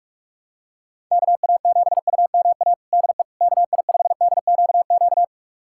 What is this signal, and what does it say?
Morse code, heard as a keyed tone, about 46 wpm, 700 Hz. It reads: KA7UMA DE KI5DQQ
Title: USA Callsign DE Callsign - 6 Characters - 45wpm